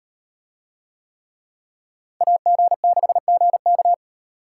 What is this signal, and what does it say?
Morse code, heard as a keyed tone, about 38 wpm, 700 Hz. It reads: AG6GK